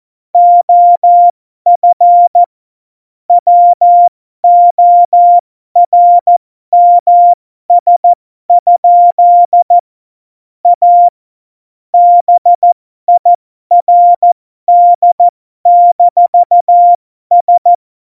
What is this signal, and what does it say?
Morse code, heard as a keyed tone, about 14 wpm, spaced as for 12 wpm, 700 Hz. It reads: OF WORMS? A BIRD-S
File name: Morsekurs / Witze